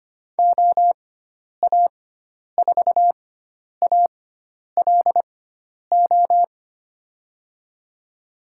Morse code, heard as a keyed tone, about 25 wpm, spaced as for 10 wpm, 700 Hz. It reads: OA4ALO
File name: Lesson 16g - Letter c - Call Signs 25wpm